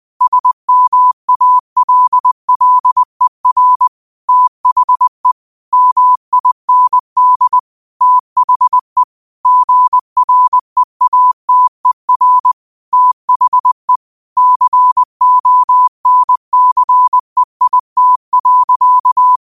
Morse code, heard as a keyed tone, about 20 wpm, 1000 Hz. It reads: SMALLER THE MIND THE GREATER THE CONCEIT.
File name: qotdfiles / 20wpm